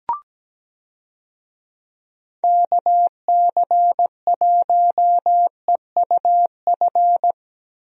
Morse code, heard as a keyed tone, about 17 wpm, 700 Hz. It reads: KC1EUF
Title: USA Call Signs - 6 Characters - Repeated 2x - 17wpm